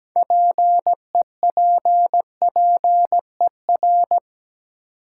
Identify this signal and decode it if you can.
Morse code, heard as a keyed tone, about 17 wpm, 700 Hz. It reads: PEPPER
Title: Sets of 5 Words - New Word Drill - 17wpm